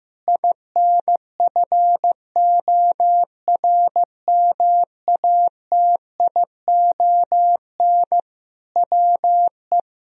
Morse code, heard as a keyed tone, about 15 wpm, 700 Hz. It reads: INFORMATION WE
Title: Sentences from Top 1000 Words - Review Part 4 - 15wpm